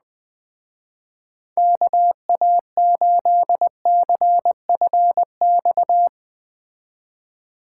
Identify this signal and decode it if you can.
Morse code, heard as a keyed tone, about 19 wpm, 700 Hz. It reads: KA8CFX